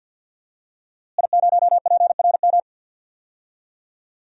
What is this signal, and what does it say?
Morse code, heard as a keyed tone, about 47 wpm, 700 Hz. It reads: I0PRM
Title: DXCC-100 Call Signs - 5 Characters - Repeated 3x - 50wpm